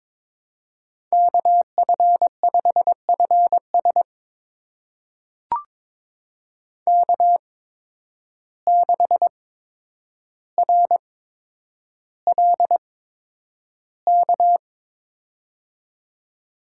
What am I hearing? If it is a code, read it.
Morse code, heard as a keyed tone, about 22 wpm, spaced as for 9 wpm, 700 Hz. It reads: KF5FS K6RLK
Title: USA Call Signs - 5 Characters - 8x Character Spacing - 22wpm